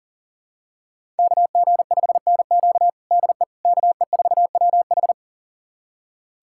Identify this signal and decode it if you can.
Morse code, heard as a keyed tone, about 40 wpm, 700 Hz. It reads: KG5NQ DE KE4WH
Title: USA Callsign DE Callsign 40wpm